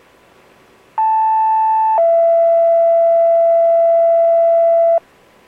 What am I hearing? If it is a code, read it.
Two-tone page, heard as a one-second tone, then a three-second tone, about 880 then 670 Hz. It is heard from 1.0 to 5.0 s.